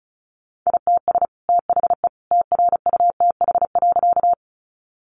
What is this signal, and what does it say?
Morse code, heard as a keyed tone, about 34 wpm, 700 Hz. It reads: ITS THE TRUTH.
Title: Sentences from Top 800 Words 35wpm